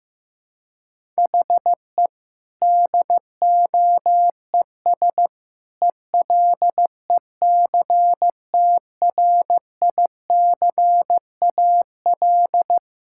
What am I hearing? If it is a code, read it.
Morse code, heard as a keyed tone, about 15 wpm, 700 Hz. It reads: HE DOES ELECTRICAL
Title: Sentences from Top 1000 Words - Review Part 6 - 15wpm